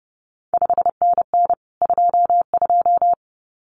Morse code, heard as a keyed tone, about 31 wpm, 700 Hz. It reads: 5NN 22
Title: CQ WW DX Contest Exchange Encore - 30wpm